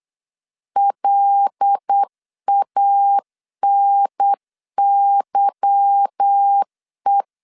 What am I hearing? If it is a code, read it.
Morse code, heard as a keyed tone, about 8 wpm, 800 Hz. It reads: LANYE
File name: LEZIONI / 8 YJ45 / 8 Y 8wpm 3245